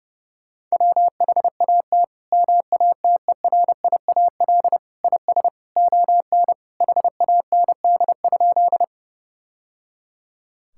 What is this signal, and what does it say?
Morse code, heard as a keyed tone, about 30 wpm, 700 Hz. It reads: WHAT MATERIAL IS ON HAND?
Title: Sentences from Top 1000 Words - Review Part 5 - 30wpm